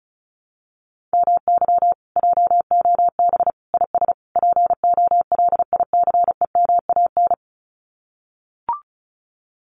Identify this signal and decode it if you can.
Morse code, heard as a keyed tone, about 34 wpm, 700 Hz. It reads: MY JOB IS POLICEMAN
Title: Common QSO Phrases Encore 35wpm